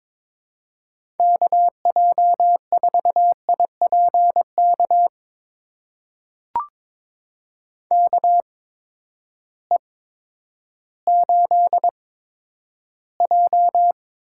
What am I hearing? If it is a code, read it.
Morse code, heard as a keyed tone, about 22 wpm, spaced as for 11 wpm, 700 Hz. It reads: KJ4IPK KE8J
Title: USA Call Signs - 8x Character Spacing - 22wpm